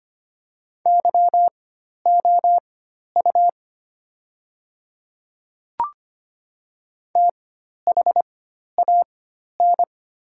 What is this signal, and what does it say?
Morse code, heard as a keyed tone, about 25 wpm, spaced as for 12 wpm, 700 Hz. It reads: YOU THAN
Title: 4x Character Spacing - Top 100 Words - 25wpm